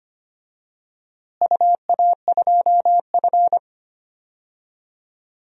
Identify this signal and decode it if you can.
Morse code, heard as a keyed tone, about 25 wpm, 700 Hz. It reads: UA2F
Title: DXCC-100 Call Signs - 4 Characters - Repeated 3x - 25wpm